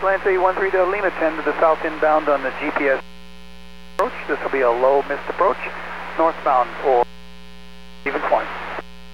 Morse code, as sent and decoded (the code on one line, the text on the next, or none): none